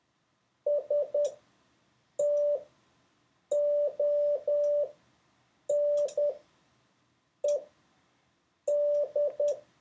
Morse code, heard as a keyed tone, about 10 wpm, spaced as for 6 wpm, 580 Hz. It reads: STONED